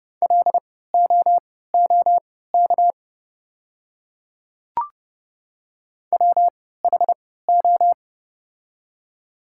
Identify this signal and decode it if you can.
Morse code, heard as a keyed tone, about 29 wpm, spaced as for 17 wpm, 690 Hz. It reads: LOOK WHO
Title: 3x Character Spacing - Top 100 Words - 30wpm